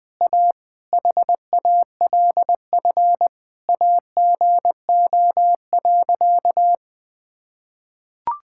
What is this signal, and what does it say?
Morse code, heard as a keyed tone, about 21 wpm, 700 Hz. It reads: A HALF AGO.
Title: Sentences with 6-Word Phrases - 20wpm